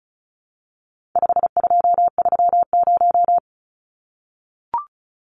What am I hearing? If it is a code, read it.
Morse code, heard as a keyed tone, about 34 wpm, 700 Hz. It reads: 5230
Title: Numbers - 4-Digit 35wpm